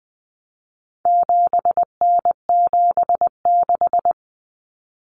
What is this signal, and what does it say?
Morse code, heard as a keyed tone, about 20 wpm, 700 Hz. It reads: 7N76
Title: Common Cut Numbers - 4-Digit 20wpm